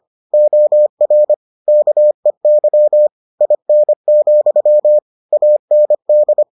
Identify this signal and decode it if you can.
Morse code, heard as a keyed tone, about 25 wpm, 600 Hz. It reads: OR KEY IN, AND